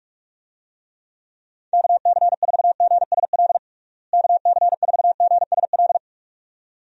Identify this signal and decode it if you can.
Morse code, heard as a keyed tone, about 42 wpm, 700 Hz. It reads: KC4GSL KC4GSL
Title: USA Call Signs - 6 Characters - Repeated 2x - 45wpm